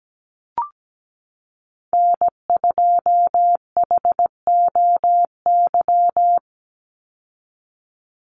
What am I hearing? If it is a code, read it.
Morse code, heard as a keyed tone, about 17 wpm, 700 Hz. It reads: N2HOY
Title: USA Call Signs 17wpm